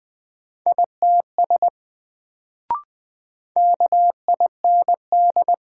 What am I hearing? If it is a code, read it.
Morse code, heard as a keyed tone, about 20 wpm, 700 Hz. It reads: ITS KIND
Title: Sets of 2 Words - New Word Drill - 20wpm